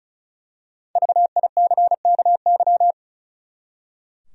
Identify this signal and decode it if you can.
Morse code, heard as a keyed tone, about 36 wpm, 700 Hz. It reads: VICKY
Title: USA Common Names - 35wpm